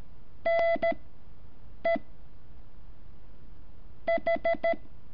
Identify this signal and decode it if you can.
Morse code, heard as a keyed tone, about 13 wpm, 670 Hz. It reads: N E H
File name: curso / ejercicio 11